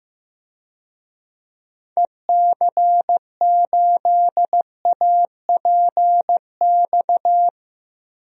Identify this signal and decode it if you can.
Morse code, heard as a keyed tone, about 15 wpm, 700 Hz. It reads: EC8APX
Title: DXCC-100 Call Signs - 6 Characters - Repeated 3x - 15wpm